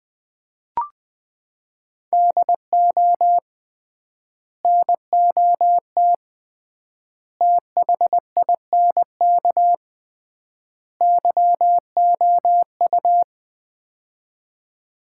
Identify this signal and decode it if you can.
Morse code, heard as a keyed tone, about 20 wpm, 700 Hz. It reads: DO NOT THINK YOU
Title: 3x Word-Spacing - Sets of 4 Words - 20wpm